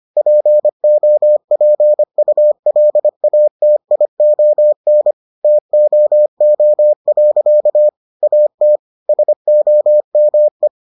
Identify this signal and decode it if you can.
Morse code, heard as a keyed tone, about 25 wpm, 600 Hz. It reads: POPULATION TOO. AT SOME